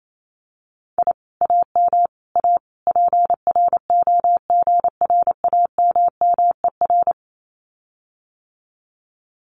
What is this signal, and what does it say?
Morse code, heard as a keyed tone, about 28 wpm, 700 Hz. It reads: I AM A PROGRAMMER
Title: Common QSO Phrases Encore 28wpm